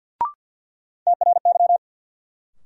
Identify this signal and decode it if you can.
Morse code, heard as a keyed tone, about 53 wpm, 700 Hz. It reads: TRY
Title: Sentences from Top 200 Words - New Word Drill - 50wpm